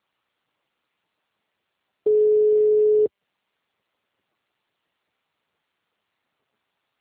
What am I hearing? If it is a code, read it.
Ringback tone (Europe, 425 Hz).
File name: out- -2005